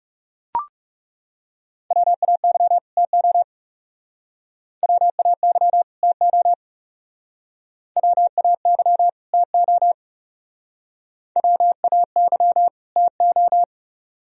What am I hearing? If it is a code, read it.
Morse code, heard as a keyed tone, about 37 wpm, 700 Hz. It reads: WAY TO WAY TO WAY TO WAY TO
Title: Speed-Racing - Sets of 2 Words 30wpm